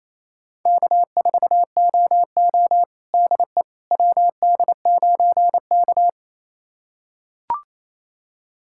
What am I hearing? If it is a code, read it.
Morse code, heard as a keyed tone, about 27 wpm, 700 Hz. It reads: K4OO DE WD9K
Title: USA Callsign DE Callsign - 4 Characters - 28wpm